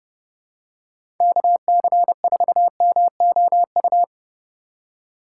Morse code, heard as a keyed tone, about 30 wpm, 700 Hz. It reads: KC4MOU
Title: USA Call Signs - 2x Character Spacing - 30wpm